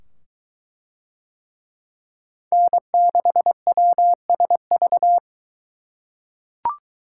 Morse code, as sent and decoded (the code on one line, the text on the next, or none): -. -.... .-- ... ...-
N6WSV